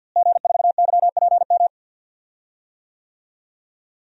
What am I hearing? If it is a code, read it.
Morse code, heard as a keyed tone, about 47 wpm, 700 Hz. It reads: G4YPM
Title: DXCC-100 Call Signs - 5 Characters - Repeated 2x - 50wpm